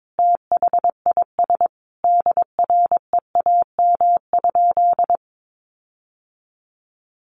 Morse code, heard as a keyed tone, about 22 wpm, 700 Hz. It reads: THIS DREAM?
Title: Sentences from Top 1500 Words 22wpm